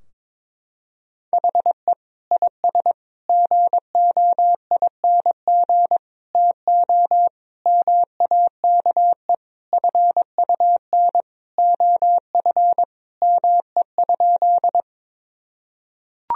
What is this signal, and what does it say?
Morse code, heard as a keyed tone, about 22 wpm, 700 Hz. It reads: HE IS GOING TO MAKE FUN OF ME?
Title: Sentences from Top 1000 Words - Review Part 4 - 22wpm